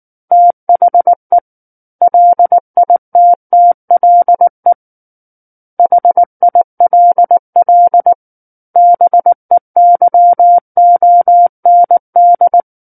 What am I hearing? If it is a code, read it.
Morse code, heard as a keyed tone, about 19 wpm, 700 Hz. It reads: THE LITTLE HILL BEYOND
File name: Book One / 10-Chapter 10 - 19 WPM